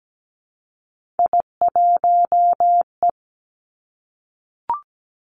Morse code, heard as a keyed tone, about 17 wpm, 700 Hz. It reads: I1E
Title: Three Challenging Letter-Number Combinations 17wpm